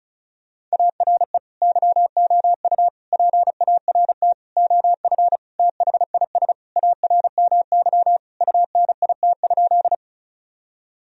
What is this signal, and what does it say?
Morse code, heard as a keyed tone, about 34 wpm, 700 Hz. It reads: ARE YOU PART OF THIS ARMY UNIT?
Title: Sentences from Top 1000 Words - Review Part 2 - 35wpm